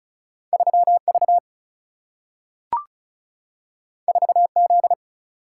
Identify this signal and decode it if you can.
Morse code, heard as a keyed tone, about 36 wpm, 700 Hz. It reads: 3 V 4 Z
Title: Single-Letter Single-Number 35wpm